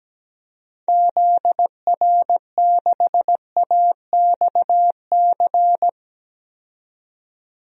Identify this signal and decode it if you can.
Morse code, heard as a keyed tone, about 17 wpm, 700 Hz. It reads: ZR6AXC
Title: DXCC-100 Call Signs - 6 Characters - 17wpm